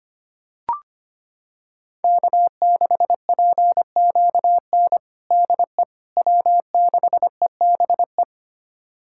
Morse code, heard as a keyed tone, about 25 wpm, 700 Hz. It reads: K6PQN DE W6EBE
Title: USA Callsign DE Callsign - 5 Characters - 25wpm